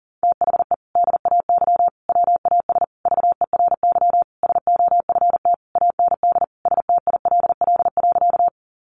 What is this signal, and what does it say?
Morse code, heard as a keyed tone, about 40 wpm, 700 Hz. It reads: THE DAY WAS VERY SOFT AND STILL.
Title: Sentences from Top 1500 Words 40wpm